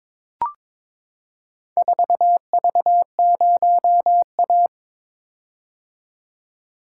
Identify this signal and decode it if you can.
Morse code, heard as a keyed tone, about 22 wpm, 700 Hz. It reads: 4V0A
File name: practice / 22wpm / Four Letter-Number Combinations - 22wpm